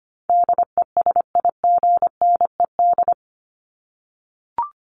Morse code, heard as a keyed tone, about 25 wpm, 700 Hz. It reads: DESIGNED
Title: ED Suffix - Top 100 Words with Base Words 25wpm